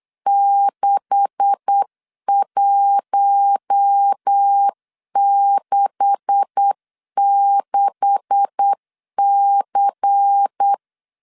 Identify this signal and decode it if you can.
Morse code, heard as a keyed tone, about 8 wpm, 800 Hz. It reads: 6166C
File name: LEZIONI / 9 286 8wpm 3425